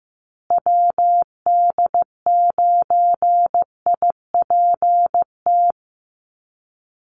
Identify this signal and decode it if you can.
Morse code, heard as a keyed tone, about 15 wpm, 700 Hz. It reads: WD9IPT
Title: USA Call Signs Repeated 3x 15wpm